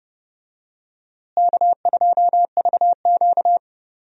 Morse code, heard as a keyed tone, about 30 wpm, 700 Hz. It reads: K2VQ
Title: USA Call Signs - 4 Characters - Repeated 2x - 30wpm